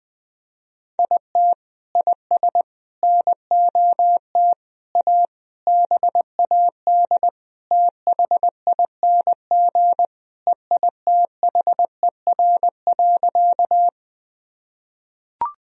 Morse code, heard as a keyed tone, about 20 wpm, 700 Hz. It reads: IT IS NOT A BAD THING EITHER.
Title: Sentences from Top 1000 Words - Review Part 4 - 20wpm